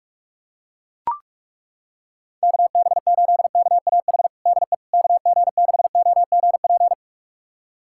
Morse code, heard as a keyed tone, about 42 wpm, 700 Hz. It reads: KB8KAH DE KG6OGP
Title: USA Callsign DE Callsign - 6 Characters - 45wpm